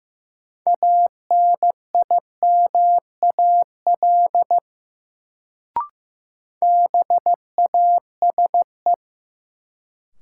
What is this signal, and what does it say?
Morse code, heard as a keyed tone, about 15 wpm, 700 Hz. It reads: ANIMAL BASE
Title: Sentences from Top 1000 Words - Review Part 1 - New Word Drill - 15wpm